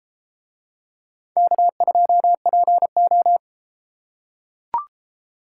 Morse code, heard as a keyed tone, about 33 wpm, 700 Hz. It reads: K2PO